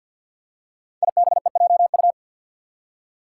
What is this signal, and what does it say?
Morse code, heard as a keyed tone, about 51 wpm, 700 Hz. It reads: IBEJU
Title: Five Challenging Letter-Number Combinations 50wpm